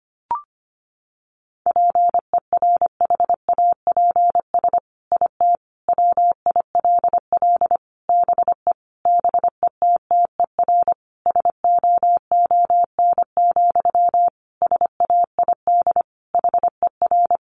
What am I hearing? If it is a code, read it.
Morse code, heard as a keyed tone, about 25 wpm, 700 Hz. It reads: PERHAPS IT WILL BE BETTER SOON, SAID HER
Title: Sentences from Top 1500 Words 25wpm